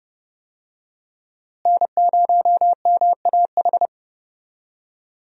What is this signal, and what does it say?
Morse code, heard as a keyed tone, about 30 wpm, 700 Hz. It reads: N0MAH
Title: USA Call Signs - 5 Characters - Repeated 3x - 30wpm